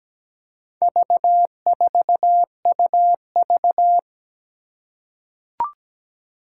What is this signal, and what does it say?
Morse code, heard as a keyed tone, about 17 wpm, 700 Hz. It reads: V4UV
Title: Four Challenging Letter-Number Combinations 17wpm